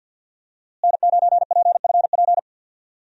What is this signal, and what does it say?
Morse code, heard as a keyed tone, about 51 wpm, 700 Hz. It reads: N9PFP
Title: USA Call Signs - 5 Characters - Repeated 2x - 50wpm